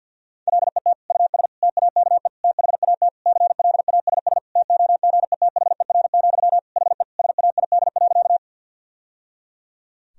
Morse code, heard as a keyed tone, about 45 wpm, 700 Hz. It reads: LET US TAKE THAT CLASS TOGETHER, HE SAID.
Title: Sentences from Top 1000 Words - Review Part 2 - 50wpm